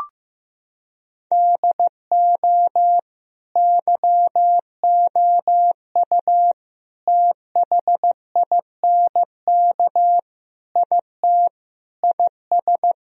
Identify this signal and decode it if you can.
Morse code, heard as a keyed tone, about 15 wpm, 700 Hz. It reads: DO YOU THINK IT IS
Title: Sentences from Top 1000 Words - Review Part 1 - 15wpm